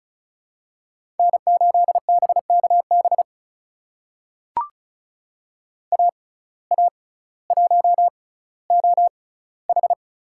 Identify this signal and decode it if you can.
Morse code, heard as a keyed tone, about 36 wpm, spaced as for 18 wpm, 700 Hz. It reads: N8BKB AA1OH
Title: USA Call Signs - 5 Characters - 6x Character Spacing - 35wpm